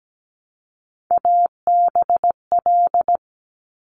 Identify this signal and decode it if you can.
Morse code, heard as a keyed tone, about 17 wpm, 700 Hz. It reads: ABL
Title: Top Three Letter Combinations in Words 17wpm